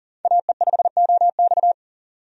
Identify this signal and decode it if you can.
Morse code, heard as a keyed tone, about 40 wpm, 700 Hz. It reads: AE5OX
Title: USA Call Signs - 5 Characters - 40wpm